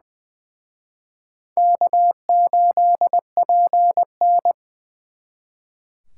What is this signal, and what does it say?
Morse code, heard as a keyed tone, about 23 wpm, 700 Hz. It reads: K8PN
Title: Speed Racing - USA 1x2 Call Signs 20wpm